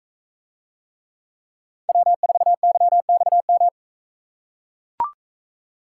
Morse code, heard as a keyed tone, about 42 wpm, 700 Hz. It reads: W4YXM